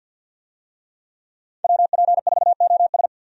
Morse code, heard as a keyed tone, about 50 wpm, 700 Hz. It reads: WP4OS
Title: USA Call Signs - 5 Characters - 50wpm